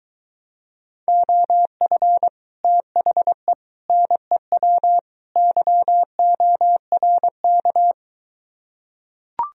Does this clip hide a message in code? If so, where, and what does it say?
Yes: Morse code from 1.1 to 7.9 s, heard as a keyed tone, about 23 wpm, 700 Hz. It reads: OF THE NEW YORK